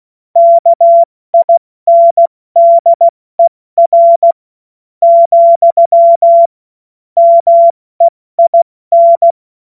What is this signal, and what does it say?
Morse code, heard as a keyed tone, about 16 wpm, spaced as for 13 wpm, 680 Hz. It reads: KINDER, MEIN
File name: Morsekurs / Witze / Witz 03. August 2021, 70 F80